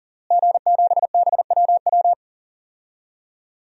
Morse code, heard as a keyed tone, about 40 wpm, 700 Hz. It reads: G7BWW
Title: DXCC-100 Call Signs - 5 Characters - Repeated 2x - 40wpm